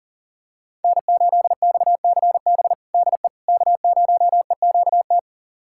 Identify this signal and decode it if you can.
Morse code, heard as a keyed tone, about 40 wpm, 700 Hz. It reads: N8XCB DE K0EQT
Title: USA Callsign DE Callsign - 5 Characters - 40wpm